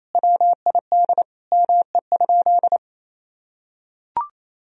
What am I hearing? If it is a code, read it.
Morse code, heard as a keyed tone, about 28 wpm, 700 Hz. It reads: WID ME?
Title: Sentences from Top 200 Words and Abbreviations 28wpm